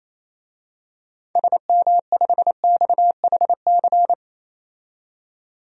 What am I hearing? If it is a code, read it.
Morse code, heard as a keyed tone, about 28 wpm, 700 Hz. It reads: SM5XHC
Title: DXCC-100 Call Signs - 6 Characters - 28wpm